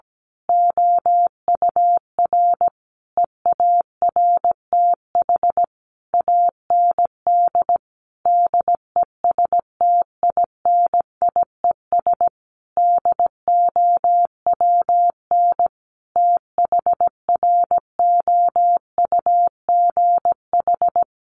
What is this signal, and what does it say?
Morse code, heard as a keyed tone, about 17 wpm, 700 Hz. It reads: OUR EARTH AND DESTINIES DOWN THROUGH
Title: A Sound Of Thunder 17wpm